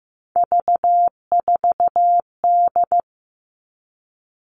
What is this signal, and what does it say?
Morse code, heard as a keyed tone, about 15 wpm, 700 Hz. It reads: V4D
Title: All Cut Numbers - 3-Digit 15wpm